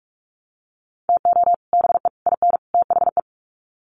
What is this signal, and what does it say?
Morse code, heard as a keyed tone, about 44 wpm, 700 Hz. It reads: TO BE IN THE
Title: Sets of 4 Words 45wpm